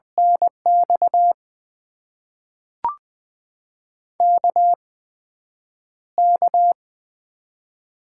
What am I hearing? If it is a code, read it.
Morse code, heard as a keyed tone, about 20 wpm, spaced as for 9 wpm, 700 Hz. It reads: NX KK
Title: USA Call Signs - 6 Characters - 8x Character Spacing - 20wpm